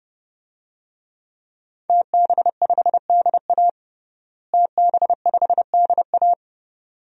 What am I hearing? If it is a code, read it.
Morse code, heard as a keyed tone, about 30 wpm, 700 Hz. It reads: TB5DA TB5DA
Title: DXCC-100 Call Signs - 5 Characters - Repeated 2x - 30wpm